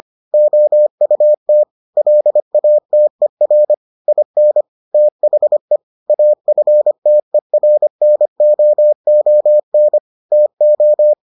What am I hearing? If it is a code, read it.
Morse code, heard as a keyed tone, about 25 wpm, 600 Hz. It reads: OUT LATER IN THE AFTERNOON TO